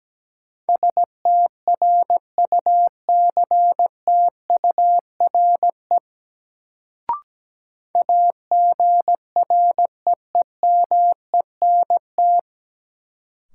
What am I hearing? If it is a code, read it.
Morse code, heard as a keyed tone, about 17 wpm, 700 Hz. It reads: STRUCTURE AGREEMENT
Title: Sentences from Top 1500 Words - New Word Drill Alt - 17wpm